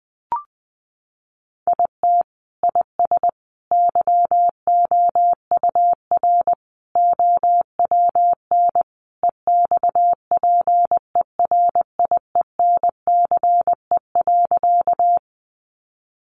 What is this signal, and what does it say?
Morse code, heard as a keyed tone, about 20 wpm, 700 Hz. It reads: IT IS YOUR OWN EXPERIENCE.
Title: Sentences from Top 600 Words 20wpm